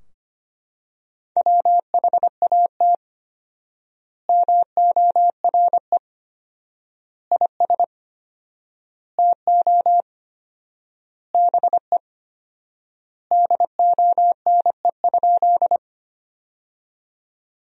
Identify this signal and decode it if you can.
Morse code, heard as a keyed tone, about 25 wpm, 700 Hz. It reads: WHAT MORE IS TO BE DONE?
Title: 4x Word-Spacing - Sentences from Top 500 Words - 25wpm